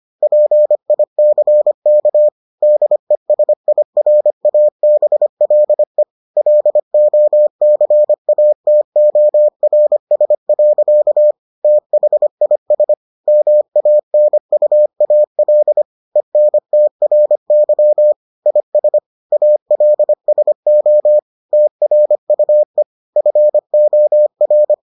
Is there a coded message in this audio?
Yes: Morse code from 0.2 to 24.7 s, heard as a keyed tone, about 25 wpm, 600 Hz. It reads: PICK DESIRABLE LOCATORS. THIS MANUAL ENTRY IS ALSO TRUE FOR